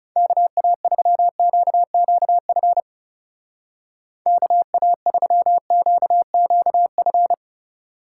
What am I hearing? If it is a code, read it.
Morse code, heard as a keyed tone, about 39 wpm, 700 Hz. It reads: KA3QQF KA3QQF